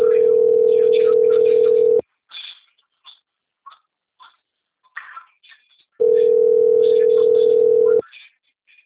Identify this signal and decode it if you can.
Ringback tone (North America), 440 + 480 Hz.